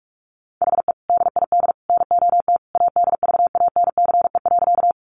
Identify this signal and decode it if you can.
Morse code, heard as a keyed tone, about 44 wpm, 700 Hz. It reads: HE DID NOT ADVANCE.
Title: Sentences from Top 1400 Words 45wpm